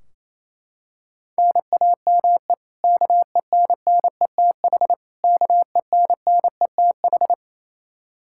Morse code, heard as a keyed tone, about 28 wpm, 700 Hz. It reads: NAME KENNETH KENNETH
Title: Practice QSO - Exchange 1 - Beginning - 28wpm